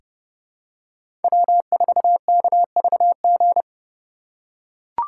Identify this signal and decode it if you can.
Morse code, heard as a keyed tone, about 30 wpm, 700 Hz. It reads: W4KVG
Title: USA Call Signs - 8x Character Spacing - 30wpm